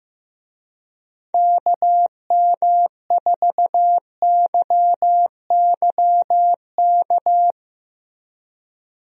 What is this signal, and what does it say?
Morse code, heard as a keyed tone, about 15 wpm, 700 Hz. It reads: KM4YYK
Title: USA Call Signs - 6 Characters - Repeated 2x - 15wpm